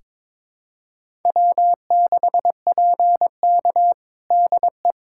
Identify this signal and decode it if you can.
Morse code, heard as a keyed tone, about 22 wpm, 700 Hz. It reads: W6PK DE
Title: USA Callsign DE Callsign - 4 Characters - 22wpm